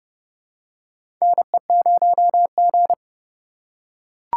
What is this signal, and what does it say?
Morse code, heard as a keyed tone, about 30 wpm, 700 Hz. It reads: NE0G
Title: USA Call Signs - 4 Characters - 3x Character Spacing - 30wpm